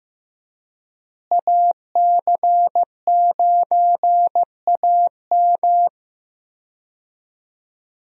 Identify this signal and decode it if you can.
Morse code, heard as a keyed tone, about 15 wpm, 700 Hz. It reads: AC9AM
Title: USA Call Signs - 5 Characters - Repeated 2x - 15wpm